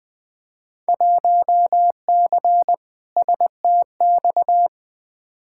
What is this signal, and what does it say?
Morse code, heard as a keyed tone, about 20 wpm, 700 Hz. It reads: 1C STX
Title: Field Day Exchanges Encore - 20wpm